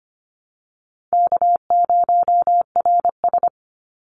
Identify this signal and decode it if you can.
Morse code, heard as a keyed tone, about 25 wpm, 700 Hz. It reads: K0RS
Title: NA SOTA Chaser Callsigns 25wpm